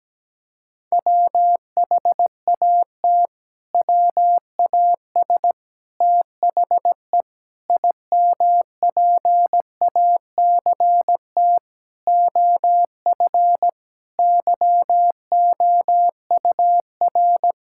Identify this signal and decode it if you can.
Morse code, heard as a keyed tone, about 17 wpm, 700 Hz. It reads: WHAT WAS THE IMPACT OF YOUR
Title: Sentences from Top 1000 Words - Review Part 6 - 17wpm